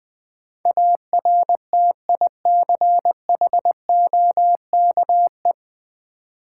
Morse code, heard as a keyed tone, about 20 wpm, 690 Hz. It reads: ARTICHOKE
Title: Difficult Words with Hints - 20wpm